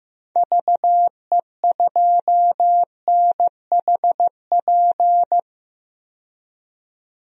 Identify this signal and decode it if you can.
Morse code, heard as a keyed tone, about 15 wpm, 700 Hz. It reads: VE2NHP
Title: DXCC-100 Call Signs - 6 Characters - 15wpm